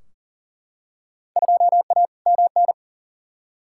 Morse code, heard as a keyed tone, about 39 wpm, 700 Hz. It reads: 2A MN
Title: Field Day Exchanges - 40wpm